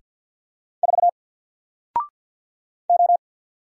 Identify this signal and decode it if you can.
Morse code, heard as a keyed tone, about 48 wpm, 700 Hz. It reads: 4 O
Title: Single Letter-Number Alt 50wpm